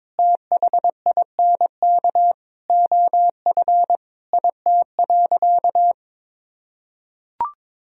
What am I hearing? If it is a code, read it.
Morse code, heard as a keyed tone, about 22 wpm, 700 Hz. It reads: THINK OF IT.
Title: Sentences from Top 200 Words and Abbreviations 22wpm